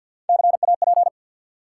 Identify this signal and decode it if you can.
Morse code, heard as a keyed tone, about 53 wpm, 700 Hz. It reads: CAP
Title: Three Letter Words - 50wpm